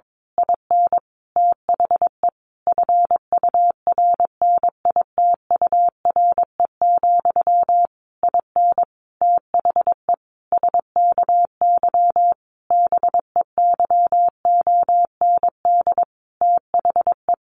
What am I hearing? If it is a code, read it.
Morse code, heard as a keyed tone, about 22 wpm, 700 Hz. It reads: IN THE FURNITURE, IN THE SKY BEYOND THE
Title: A Sound Of Thunder 22wpm